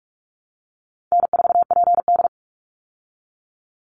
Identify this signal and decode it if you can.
Morse code, heard as a keyed tone, about 44 wpm, 700 Hz. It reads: N4PD
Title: Easy Call Signs Repeated 3x - 45wpm